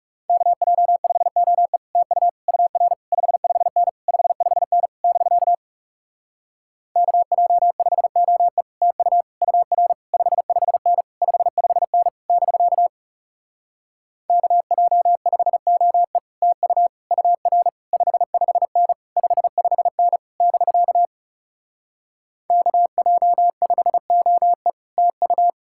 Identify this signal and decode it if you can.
Morse code, heard as a keyed tone, about 40 wpm, 700 Hz. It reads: KJ5OE TU UR 55N 55N <BK> KJ5OE TU UR 55N 55N <BK> KJ5OE TU UR 55N 55N <BK> KJ5OE TU